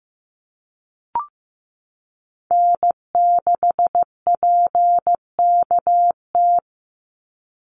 Morse code, heard as a keyed tone, about 15 wpm, 700 Hz. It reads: N6PKT
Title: NA SOTA Chaser Callsigns 15wpm